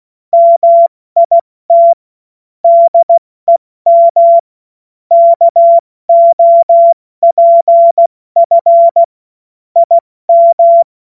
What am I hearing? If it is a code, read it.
Morse code, heard as a keyed tone, about 16 wpm, 680 Hz. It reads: MIT DEM KOPF IM
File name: Morsekurs / Witze / Witz 17.Oktober 2023 70 F80